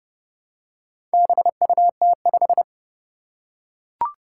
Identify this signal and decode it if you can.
Morse code, heard as a keyed tone, about 30 wpm, 700 Hz. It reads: BUT5
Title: Four Challenging Letter-Number Combinations 30wpm